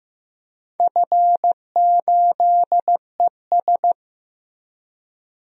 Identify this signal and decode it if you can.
Morse code, heard as a keyed tone, about 15 wpm, 700 Hz. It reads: F8ES
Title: DXCC-100 Call Signs - 4 Characters - Repeated 2x - 15wpm